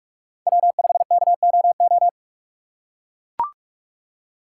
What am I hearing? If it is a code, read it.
Morse code, heard as a keyed tone, about 43 wpm, 700 Hz. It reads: W5KOO
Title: USA Call Signs - 5 Characters - 4x Character Spacing - 45wpm